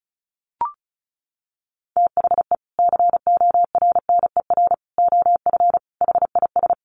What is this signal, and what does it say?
Morse code, heard as a keyed tone, about 34 wpm, 700 Hz. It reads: THE CORNER OF HIS
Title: Sets of 5 Words Encore2 35wpm